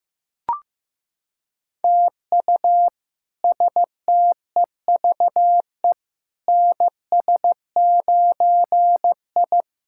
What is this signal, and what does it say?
Morse code, heard as a keyed tone, about 15 wpm, 700 Hz. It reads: TU STEVE NS9I
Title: K1USN SST QSO Practice - 15wpm